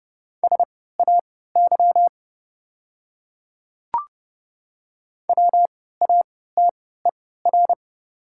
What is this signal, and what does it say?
Morse code, heard as a keyed tone, about 30 wpm, spaced as for 17 wpm, 690 Hz. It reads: SAY WATER
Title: 3x Character Spacing - Top 200 Words - 30wpm